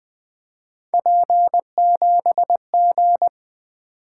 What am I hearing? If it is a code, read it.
Morse code, heard as a keyed tone, about 21 wpm, 700 Hz. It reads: P7G
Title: Three Letter-Number Combinations - 20wpm